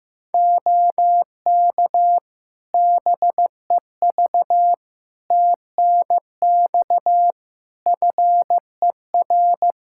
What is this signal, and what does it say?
Morse code, heard as a keyed tone, about 16 wpm, 700 Hz. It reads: OK BEV TNX FER
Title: Practice QSO - Exchange 4 - Ending - 15wpm